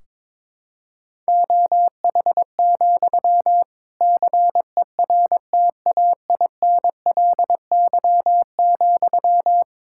Morse code, heard as a keyed tone, about 22 wpm, 700 Hz. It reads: OH, CERTAINLY,
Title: Sentences from Top 1600 Words 22wpm